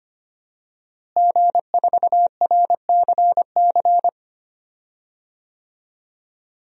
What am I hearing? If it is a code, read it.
Morse code, heard as a keyed tone, about 25 wpm, 700 Hz. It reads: G4RCC
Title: DXCC-100 Call Signs - 5 Characters - Repeated 2x - 25wpm